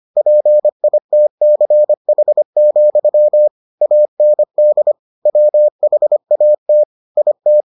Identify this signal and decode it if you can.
Morse code, heard as a keyed tone, about 25 wpm, 600 Hz. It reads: PITCH, AND WHAT IT